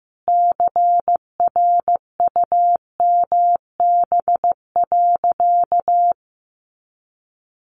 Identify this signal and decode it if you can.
Morse code, heard as a keyed tone, about 15 wpm, 700 Hz. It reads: CRUMB.
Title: Easy Phrases 15wpm